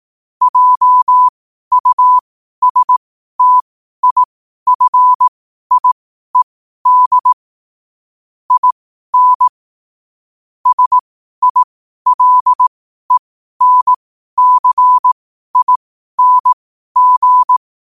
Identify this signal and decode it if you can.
Morse code, heard as a keyed tone, about 18 wpm, spaced as for 12 wpm, 1000 Hz. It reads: JUSTIFIED IN SILENCING
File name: qotdfiles / 12wpm